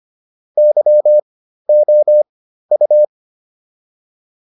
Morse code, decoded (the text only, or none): YOU